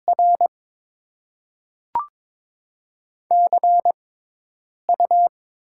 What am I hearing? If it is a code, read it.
Morse code, heard as a keyed tone, about 22 wpm, spaced as for 8 wpm, 690 Hz. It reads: R CU
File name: Top 20 QSO Elements - 6x Character Spacing - 22wpm